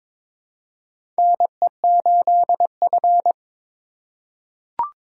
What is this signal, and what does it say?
Morse code, heard as a keyed tone, about 22 wpm, 700 Hz. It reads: NE8F
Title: USA Call Signs - 4 Characters - 4x Character Spacing - 22wpm